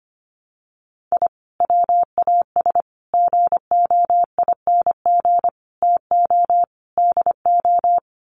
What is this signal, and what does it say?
Morse code, heard as a keyed tone, about 25 wpm, 700 Hz. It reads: I WAS GOING TO DO
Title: Sets of 5 Words Encore2 25wpm